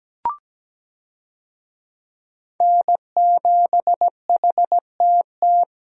Morse code, heard as a keyed tone, about 17 wpm, 700 Hz. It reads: N7HTT
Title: USA Call Signs - 5 Characters - Repeated 2x - 17wpm